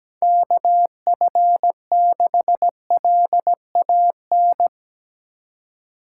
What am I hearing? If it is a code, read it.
Morse code, heard as a keyed tone, about 17 wpm, 700 Hz. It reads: KF6LAN
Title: USA Call Signs - 6 Characters - 3x Character Spacing - 17wpm